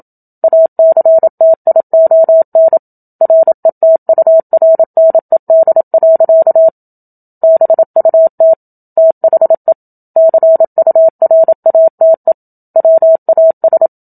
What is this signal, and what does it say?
Morse code, heard as a keyed tone, about 27 wpm, 650 Hz. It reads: ACTION RETURNED. BUT THE CURATE WAS
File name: Book Two / 01-Chapter 1 - 27 WPM